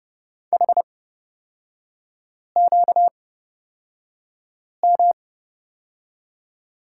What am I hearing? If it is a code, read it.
Morse code, heard as a keyed tone, about 31 wpm, spaced as for 5 wpm, 700 Hz. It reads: HQM
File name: Lesson 38e - Letter q - Three Characters 30wpm